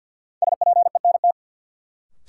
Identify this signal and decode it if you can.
Morse code, heard as a keyed tone, about 48 wpm, 700 Hz. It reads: SPENT
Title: Sets of 5 Words - New Word Drill - 50wpm